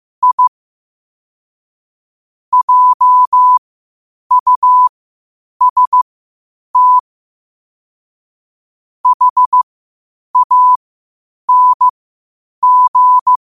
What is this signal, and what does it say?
Morse code, heard as a keyed tone, about 15 wpm, spaced as for 8 wpm, 1000 Hz. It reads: I JUST HANG
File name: qotdfiles / 8wpm